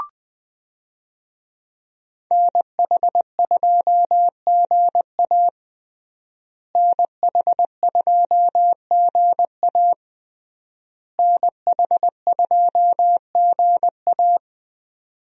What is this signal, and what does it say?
Morse code, heard as a keyed tone, about 20 wpm, 700 Hz. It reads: NH2GA NH2GA NH2GA
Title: USA Call Signs - 5 Characters - Repeated 3x - 20wpm